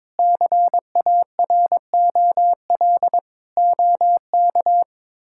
Morse code, heard as a keyed tone, about 22 wpm, 700 Hz. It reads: CAROL OK
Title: K1USN SST QSO Practice - 22wpm